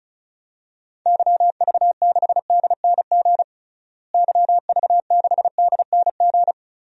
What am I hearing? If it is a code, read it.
Morse code, heard as a keyed tone, about 36 wpm, 700 Hz. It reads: YV6DNG YV6DNG
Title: DXCC-100 Call Signs - 6 Characters - Repeated 2x - 35wpm